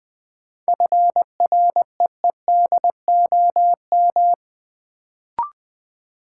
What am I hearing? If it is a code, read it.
Morse code, heard as a keyed tone, about 20 wpm, 700 Hz. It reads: FREEDOM
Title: Sentences from Top 800 Words - New Word Drill - 20wpm